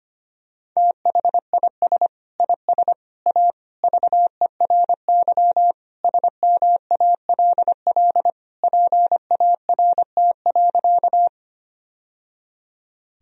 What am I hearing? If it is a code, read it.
Morse code, heard as a keyed tone, about 25 wpm, 700 Hz. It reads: THIS IS A VERY SMALL PART.
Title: Sentences from Top 1000 Words - Review Part 1 - 25wpm